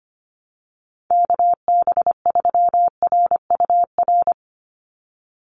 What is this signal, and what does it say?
Morse code, heard as a keyed tone, about 25 wpm, 700 Hz. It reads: KB3RUR
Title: USA Call Signs 25wpm